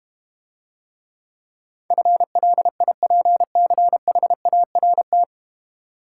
Morse code, heard as a keyed tone, about 30 wpm, 700 Hz. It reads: FLIPCHART
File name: Easy Words - Speed-Racing with Hints - 25wpm